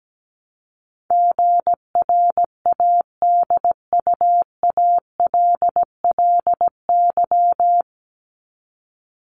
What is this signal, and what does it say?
Morse code, heard as a keyed tone, about 17 wpm, 700 Hz. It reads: GRADUALLY
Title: LY Suffix - Top 100 Words with Base Words 17wpm